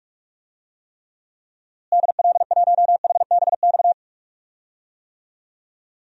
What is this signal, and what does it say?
Morse code, heard as a keyed tone, about 43 wpm, 700 Hz. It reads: DL1HBX
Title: DXCC-100 Call Signs - 6 Characters - Repeated 2x - 45wpm